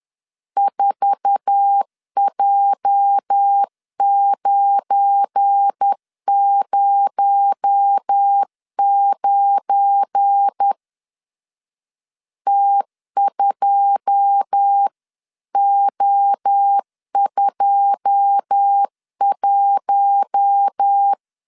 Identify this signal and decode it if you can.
Morse code, heard as a keyed tone, about 10 wpm, 800 Hz. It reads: 4J909 T2O21